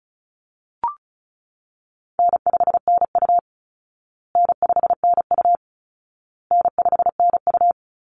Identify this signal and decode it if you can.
Morse code, heard as a keyed tone, about 34 wpm, 700 Hz. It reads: N5NU N5NU N5NU
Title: Easy Call Signs Repeated 3x - 35wpm